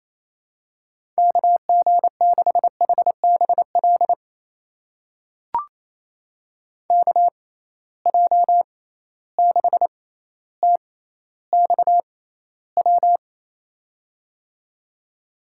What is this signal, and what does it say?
Morse code, heard as a keyed tone, about 28 wpm, spaced as for 14 wpm, 700 Hz. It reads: KG6HBL KJ6TXW
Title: USA Call Signs - 6 Characters - 6x Character Spacing - 28wpm